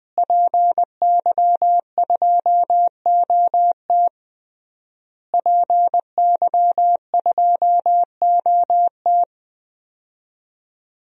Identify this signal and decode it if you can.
Morse code, heard as a keyed tone, about 21 wpm, 700 Hz. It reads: PY2OT PY2OT
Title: DXCC-100 Call Signs - 5 Characters - Repeated 3x - 20wpm